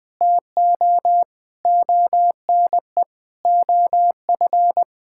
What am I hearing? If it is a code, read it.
Morse code, heard as a keyed tone, about 20 wpm, 700 Hz. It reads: TO ONE OF
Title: Sentences with 4-Word Phrases - Spoken Beginning - 20wpm